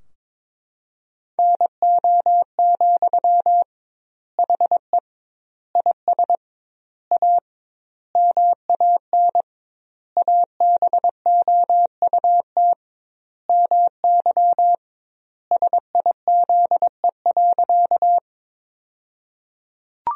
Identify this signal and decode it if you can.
Morse code, heard as a keyed tone, about 22 wpm, 700 Hz. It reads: NO, HE IS A MAN ABOUT MY SIZE.
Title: 2x Word-Spacing - Sentences from Top 400 Words - 22wpm